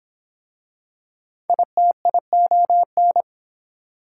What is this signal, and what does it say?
Morse code, heard as a keyed tone, about 22 wpm, 700 Hz. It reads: ITION